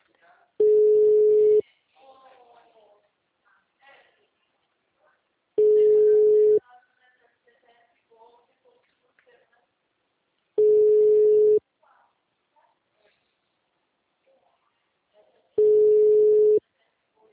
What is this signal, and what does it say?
Ringback tone (Europe, 425 Hz), with speech.